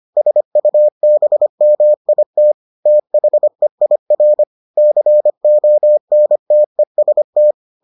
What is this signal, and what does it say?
Morse code, heard as a keyed tone, about 25 wpm, 600 Hz. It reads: SUBMIT THEIR CONTEST